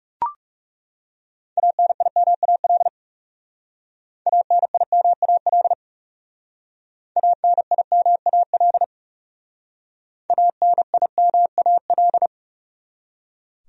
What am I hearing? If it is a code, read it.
Morse code, heard as a keyed tone, about 36 wpm, 700 Hz. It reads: ANIMAL ANIMAL ANIMAL ANIMAL
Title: Speed-Racing Top 500 Words 30wpm